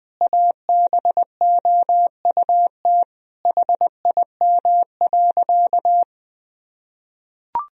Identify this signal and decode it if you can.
Morse code, heard as a keyed tone, about 20 wpm, 700 Hz. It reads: ABOUT HIM.
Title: Sentences from Top 1000 Words - Review Part 3 - 20wpm